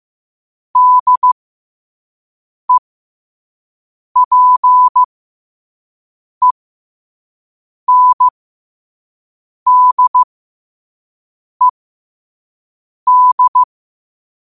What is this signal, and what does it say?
Morse code, heard as a keyed tone, about 15 wpm, spaced as for 5 wpm, 1000 Hz. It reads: DEPENDED
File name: qotdfiles / 5wpm